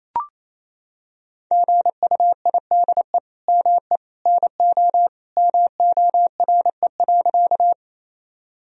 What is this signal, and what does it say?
Morse code, heard as a keyed tone, about 28 wpm, 700 Hz. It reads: GUIDE ME NO MORE.
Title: Sentences from Top 1000 Words - Review Part 3 - 28wpm